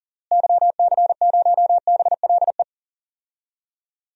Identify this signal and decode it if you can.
Morse code, heard as a keyed tone, about 40 wpm, 700 Hz. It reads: YC0BLE
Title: DXCC-100 Call Signs - 6 Characters - Repeated 3x - 40wpm